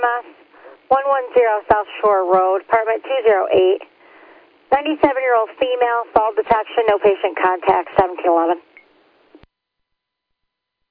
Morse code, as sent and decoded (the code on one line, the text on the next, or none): none